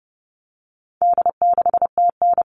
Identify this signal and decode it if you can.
Morse code, heard as a keyed tone, about 30 wpm, 700 Hz. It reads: D6TN
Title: All Cut Numbers - 4-Digit 30wpm